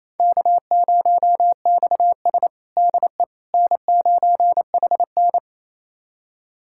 Morse code, heard as a keyed tone, about 27 wpm, 700 Hz. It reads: K0XS DE N9HN
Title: USA Callsign DE Callsign - 4 Characters - 28wpm